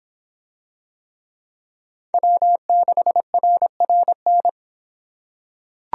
Morse code, heard as a keyed tone, about 26 wpm, 700 Hz. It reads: W6RRN